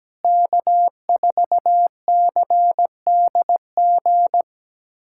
Morse code, heard as a keyed tone, about 17 wpm, 700 Hz. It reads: K4CDG
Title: USA Callsign DE Callsign 17wpm